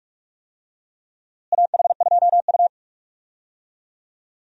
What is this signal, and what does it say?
Morse code, heard as a keyed tone, about 46 wpm, 700 Hz. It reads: AH2U
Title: USA Call Signs - 4 Characters - Repeated 3x - 45wpm